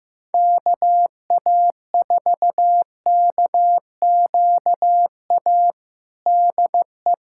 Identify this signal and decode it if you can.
Morse code, heard as a keyed tone, about 15 wpm, 700 Hz. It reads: KA4KQA DE
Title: USA Callsign DE Callsign 15wpm